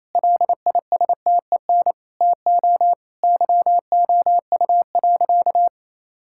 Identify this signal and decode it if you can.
Morse code, heard as a keyed tone, about 28 wpm, 700 Hz. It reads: LISTEN TO YOU.
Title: Sentences from Top 1600 Words 28wpm